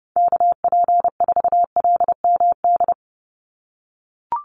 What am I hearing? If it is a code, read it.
Morse code, heard as a keyed tone, about 30 wpm, 700 Hz. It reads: KP4LMD
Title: USA Call Signs 30wpm